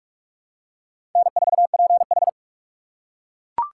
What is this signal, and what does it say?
Morse code, heard as a keyed tone, about 41 wpm, 700 Hz. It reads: N4PH
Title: USA Call Signs - 4 Characters - 3x Character Spacing - 45wpm